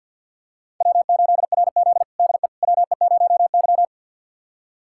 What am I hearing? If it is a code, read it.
Morse code, heard as a keyed tone, about 52 wpm, 700 Hz. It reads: W8RZ DE WE0Y
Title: USA Callsign DE Callsign - 4 Characters - 50wpm